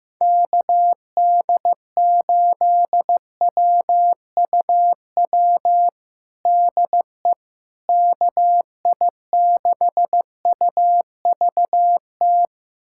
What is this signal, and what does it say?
Morse code, heard as a keyed tone, about 15 wpm, 700 Hz. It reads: KD8WUW DE KI6UVT
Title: USA Callsign DE Callsign - 6 Characters - 15wpm